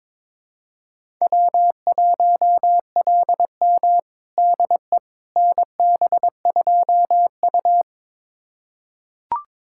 Morse code, heard as a keyed tone, about 22 wpm, 700 Hz. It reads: W1LM DE NB2U
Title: USA Callsign DE Callsign - 4 Characters - 22wpm